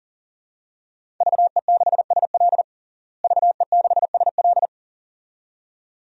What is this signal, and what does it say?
Morse code, heard as a keyed tone, about 37 wpm, 700 Hz. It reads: VE6SL VE6SL
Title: DXCC-100 Call Signs - 5 Characters - Repeated 2x - 40wpm